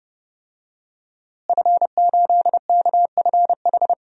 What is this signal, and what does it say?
Morse code, heard as a keyed tone, about 30 wpm, 700 Hz. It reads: F8KFH
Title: DXCC-100 Call Signs - 5 Characters - Repeated 2x - 30wpm